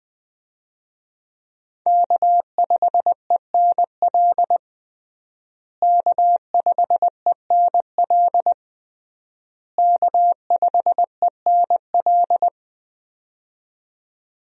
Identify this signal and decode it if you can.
Morse code, heard as a keyed tone, about 20 wpm, 700 Hz. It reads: K5ENL K5ENL K5ENL
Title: USA Call Signs - 5 Characters - Repeated 3x - 20wpm